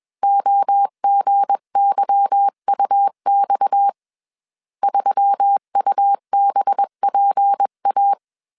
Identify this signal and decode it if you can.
Morse code, heard as a keyed tone, about 21 wpm, 800 Hz. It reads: OGYUX 3UBPA